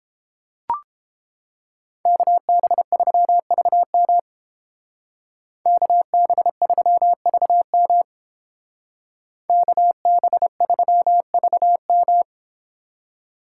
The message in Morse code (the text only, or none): KB3VM KB3VM KB3VM